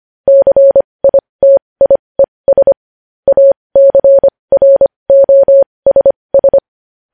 Morse code, heard as a keyed tone, about 25 wpm, spaced as for 20 wpm, 550 Hz. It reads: CITIES ACROSS
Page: News Headlines in Morse Code at 20 WPM